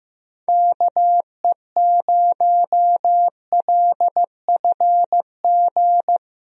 Morse code, heard as a keyed tone, about 15 wpm, 700 Hz. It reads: KE0LFG
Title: USA Call Signs - 6 Characters - Repeated 2x - 15wpm